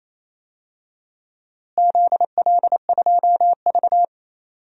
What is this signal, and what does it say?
Morse code, heard as a keyed tone, about 28 wpm, 700 Hz. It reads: ZL2V
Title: DXCC-100 Call Signs - 4 Characters - Repeated 2x - 28wpm